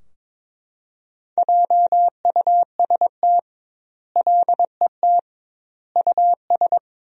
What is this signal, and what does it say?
Morse code, heard as a keyed tone, about 22 wpm, 700 Hz. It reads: JUST LET US
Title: 2x Word-Spacing - Sentences from Top 500 Words - 22wpm